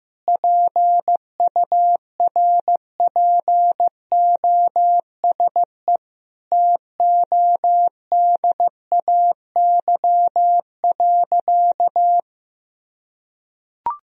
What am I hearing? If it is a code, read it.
Morse code, heard as a keyed tone, about 15 wpm, 700 Hz. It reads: PURPOSE TODAY.
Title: Sentences from Top 1000 Words - Review Part 4 - 15wpm